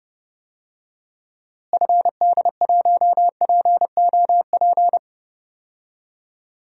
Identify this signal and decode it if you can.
Morse code, heard as a keyed tone, about 32 wpm, 700 Hz. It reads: FD1POP
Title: DXCC-100 Call Signs - 6 Characters - Repeated 2x - 30wpm